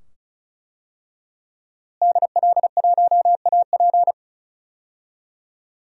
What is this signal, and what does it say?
Morse code, heard as a keyed tone, about 36 wpm, 700 Hz. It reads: DL1AP
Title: DXCC-100 Call Signs - 5 Characters - Repeated 2x - 35wpm